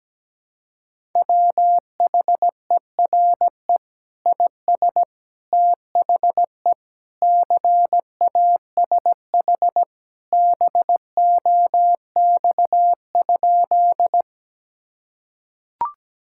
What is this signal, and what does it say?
Morse code, heard as a keyed tone, about 17 wpm, 700 Hz. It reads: WHERE IS THE CASH BOX?
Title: Sentences from Top 1000 Words - Review Part 5 - 17wpm